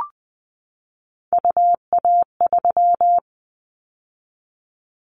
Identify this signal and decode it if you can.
Morse code, heard as a keyed tone, about 20 wpm, 700 Hz. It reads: UA3
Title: Callsign Prefixes for Top 25 DXCC - 20wpm